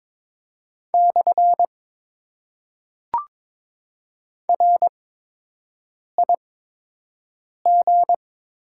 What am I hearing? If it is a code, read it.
Morse code, heard as a keyed tone, about 22 wpm, spaced as for 6 wpm, 700 Hz. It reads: / RIG
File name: Top 90 QSO Elements - 8x Character Spacing - 22wpm